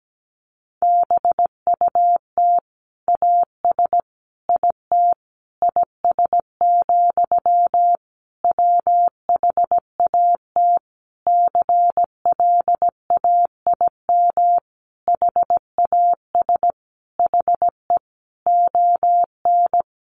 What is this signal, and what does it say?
Morse code, heard as a keyed tone, about 17 wpm, 700 Hz. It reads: BUT AS IT IS, WHAT CLAIM HAS HE ON
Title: Sentences from Top 1500 Words 17wpm